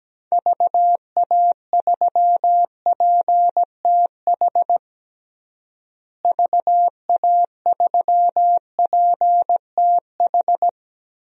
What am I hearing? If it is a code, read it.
Morse code, heard as a keyed tone, about 17 wpm, 700 Hz. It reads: VA3PTH VA3PTH
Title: DXCC-100 Call Signs - 6 Characters - Repeated 2x - 17wpm